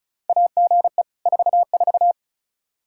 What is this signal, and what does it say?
Morse code, heard as a keyed tone, about 36 wpm, 700 Hz. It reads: AGE 44
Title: Practice QSO - Exchange 3 - More Info - 35wpm